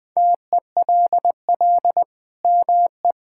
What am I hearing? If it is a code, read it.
Morse code, heard as a keyed tone, about 21 wpm, 700 Hz. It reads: TELL ME
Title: Sentences from Top 1000 Words - Review Part 5 - 20wpm